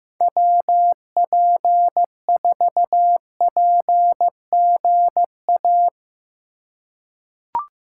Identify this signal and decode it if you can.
Morse code, heard as a keyed tone, about 15 wpm, 700 Hz. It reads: WP4PGA
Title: USA Call Signs - 6 Characters - 4x Character Spacing - 15wpm